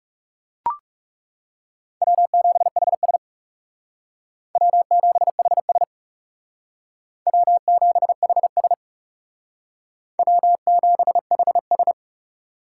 Morse code, heard as a keyed tone, about 36 wpm, 700 Hz. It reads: W7HS W7HS W7HS W7HS
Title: Speed Racing - USA 1x2 Call Signs 30wpm